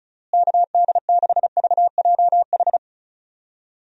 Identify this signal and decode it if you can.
Morse code, heard as a keyed tone, about 34 wpm, 700 Hz. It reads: KD6VJH
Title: USA Call Signs - 6 Characters - Repeated 2x - 35wpm